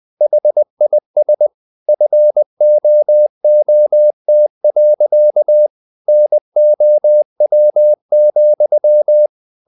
Morse code, heard as a keyed tone, about 20 wpm, 600 Hz. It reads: HIS FOOT. NOW,
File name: the old man and the sea part 2 20wpm